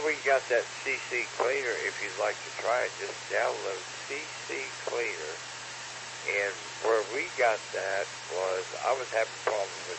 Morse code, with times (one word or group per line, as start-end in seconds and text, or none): none